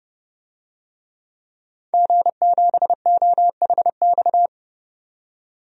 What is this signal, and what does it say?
Morse code, heard as a keyed tone, about 30 wpm, 700 Hz. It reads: G7OHX
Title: DXCC-100 Call Signs - 5 Characters - 30wpm